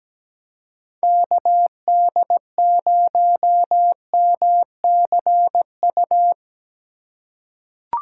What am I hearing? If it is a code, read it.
Morse code, heard as a keyed tone, about 17 wpm, 700 Hz. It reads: KD0MCU
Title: USA Call Signs - 2x Character Spacing - 17wpm